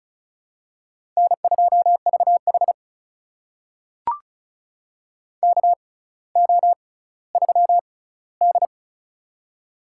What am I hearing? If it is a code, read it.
Morse code, heard as a keyed tone, about 36 wpm, spaced as for 18 wpm, 700 Hz. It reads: N2VH KO3D
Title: USA Call Signs - 4 Characters - 6x Character Spacing - 35wpm